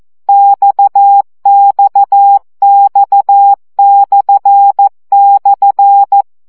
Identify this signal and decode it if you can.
Morse code, heard as a keyed tone, about 14 wpm, 800 Hz. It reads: XXX//
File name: CW gyakorlas x per 12 baud